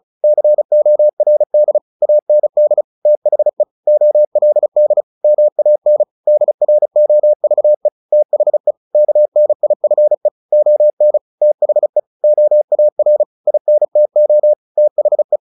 Morse code, heard as a keyed tone, about 34 wpm, 600 Hz. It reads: CORD AND THE OLD MAN DROVE THE KNIFE ON THE OAR INTO THE
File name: the old man and the sea 2 35wpm